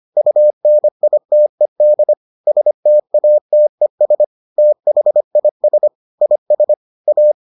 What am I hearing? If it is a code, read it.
Morse code, heard as a keyed tone, about 25 wpm, 600 Hz. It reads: UNITED STATES THIS IS A